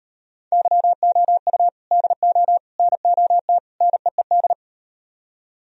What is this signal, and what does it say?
Morse code, heard as a keyed tone, about 37 wpm, 700 Hz. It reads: YOU DO NOT NEED